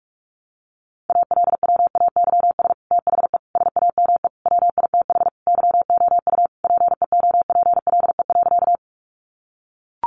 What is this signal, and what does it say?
Morse code, heard as a keyed tone, about 44 wpm, 700 Hz. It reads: ALWAYS THE SAME WITH YOU PEOPLE.
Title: Sentences from Top 300 Words 45wpm